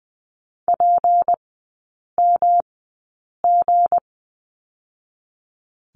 Morse code, heard as a keyed tone, about 20 wpm, spaced as for 8 wpm, 700 Hz. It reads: PMG
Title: 159 Lesson 26e - Letter g - Three Characters 20wpm